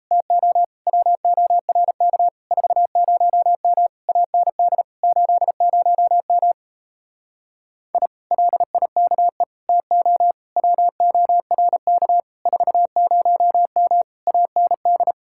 Morse code, heard as a keyed tone, about 38 wpm, 700 Hz. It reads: TO WORK 40M AND 80M I LIKE TO WORK 40M AND